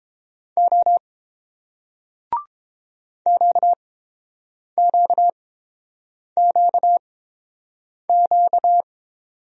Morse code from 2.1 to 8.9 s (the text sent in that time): QQQQ